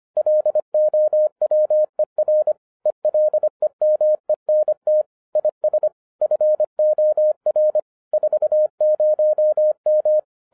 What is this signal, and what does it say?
Morse code, heard as a keyed tone, about 25 wpm, 610 Hz. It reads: LOWER ELEMENT IS FOR 40M